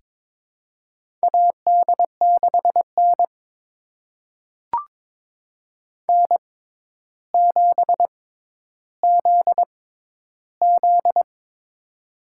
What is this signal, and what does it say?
Morse code, heard as a keyed tone, about 21 wpm, spaced as for 11 wpm, 700 Hz. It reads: AD6N N7ZZ
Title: USA Call Signs - 4 Characters - 6x Character Spacing - 22wpm